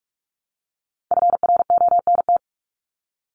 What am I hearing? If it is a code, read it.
Morse code, heard as a keyed tone, about 44 wpm, 700 Hz. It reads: FRONT
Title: Top 500 Words Repeated 45wpm